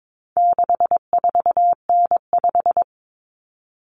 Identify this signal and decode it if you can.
Morse code, heard as a keyed tone, about 22 wpm, 700 Hz. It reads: 64N5
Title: Common Cut Numbers - 4-Digit 22wpm